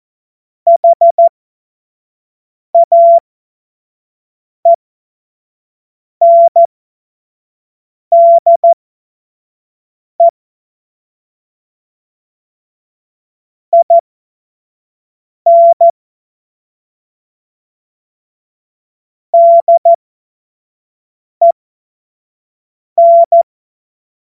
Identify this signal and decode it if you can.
Morse code, heard as a keyed tone, about 14 wpm, spaced as for 5 wpm, 680 Hz. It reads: HAENDE IN DEN
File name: Morsekurs / Witze / Witz 03. Mai 2022 25 F70